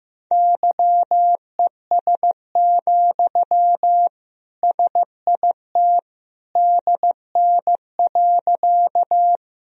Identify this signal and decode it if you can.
Morse code, heard as a keyed tone, about 15 wpm, 700 Hz. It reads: YES, SIT DN.
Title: Sentences from Top 500 Words and Abbreviations 15wpm